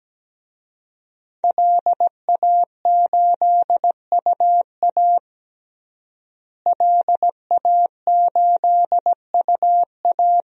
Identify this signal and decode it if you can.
Morse code, heard as a keyed tone, about 17 wpm, 700 Hz. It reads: LA8UA LA8UA
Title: DXCC-100 Call Signs - 5 Characters - Repeated 3x - 17wpm